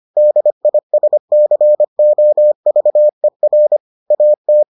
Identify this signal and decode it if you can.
Morse code, heard as a keyed tone, about 25 wpm, 600 Hz. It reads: DISCOVER AT